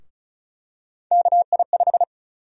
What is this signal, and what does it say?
Morse code, heard as a keyed tone, about 36 wpm, 700 Hz. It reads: KI5
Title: USA Call Sign Prefixes - Repeated 2x - 35wpm